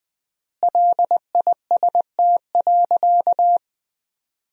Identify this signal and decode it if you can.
Morse code, heard as a keyed tone, about 21 wpm, 700 Hz. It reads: LIST.
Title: 3x Word-Spacing - Sentences from Top 400 Words - 20wpm